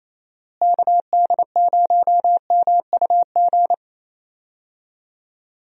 Morse code, heard as a keyed tone, about 28 wpm, 700 Hz. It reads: KD0MUG
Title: USA Call Signs - 6 Characters - Repeated 3x - 28wpm